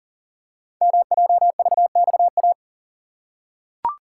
Morse code, heard as a keyed tone, about 39 wpm, 700 Hz. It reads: MJVXA
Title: Code Groups - 5 Letters - 40wpm